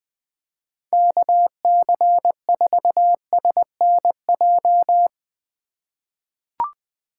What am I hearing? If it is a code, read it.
Morse code, heard as a keyed tone, about 20 wpm, 700 Hz. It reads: KC4SNJ
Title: USA Call Signs - 6 Characters - 2x Character Spacing - 20wpm